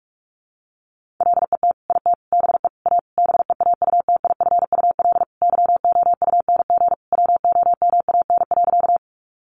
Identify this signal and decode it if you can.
Morse code, heard as a keyed tone, about 44 wpm, 700 Hz. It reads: LET IT BE A BEAUTIFUL YOUNG WOMAN.
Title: Sentences from Top 1100 Words 45wpm